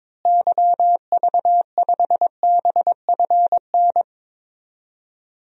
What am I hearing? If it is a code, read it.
Morse code, heard as a keyed tone, about 22 wpm, 700 Hz. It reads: YV5BFN
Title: DXCC-100 Call Signs - 6 Characters - Repeated 2x - 22wpm